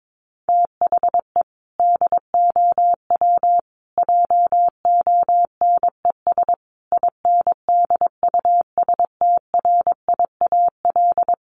Morse code, heard as a keyed tone, about 22 wpm, 700 Hz. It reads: THE DOW JONES INDUSTRIAL
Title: Sets of 5 Words 22wpm